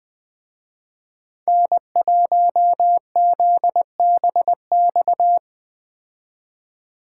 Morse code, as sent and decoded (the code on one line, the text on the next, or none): -. .---- --.. -... -..-
N1ZBX